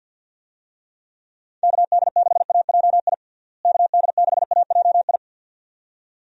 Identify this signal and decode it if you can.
Morse code, heard as a keyed tone, about 46 wpm, 700 Hz. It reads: KD6AJI KD6AJI
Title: USA Call Signs - 6 Characters - Repeated 2x - 50wpm